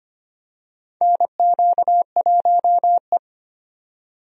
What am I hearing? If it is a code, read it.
Morse code, heard as a keyed tone, about 25 wpm, 700 Hz. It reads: NQ1E
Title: USA Call Signs - 4 Characters - 25wpm